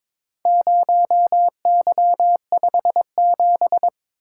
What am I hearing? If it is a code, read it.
Morse code, heard as a keyed tone, about 22 wpm, 700 Hz. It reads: 0Y57
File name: practice / 22wpm / Four Letter-Number Combinations - 22wpm